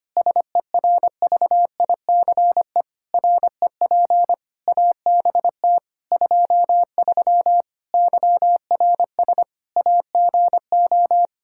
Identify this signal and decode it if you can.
Morse code, heard as a keyed tone, about 25 wpm, 700 Hz. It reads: SERVICE REP ABT 23 YRS AGO
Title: K1USN SST QSO Practice 25wpm